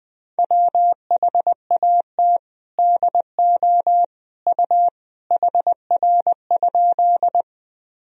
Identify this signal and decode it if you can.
Morse code, heard as a keyed tone, about 20 wpm, 700 Hz. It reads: WHAT DO U HR?
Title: Sentences from Top 300 Words and Abbreviations 20wpm